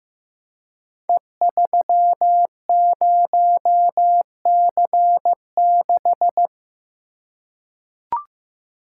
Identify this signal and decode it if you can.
Morse code, heard as a keyed tone, about 15 wpm, 700 Hz. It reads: E30C6
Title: Code Groups - 5 Letter:Numbers - 15wpm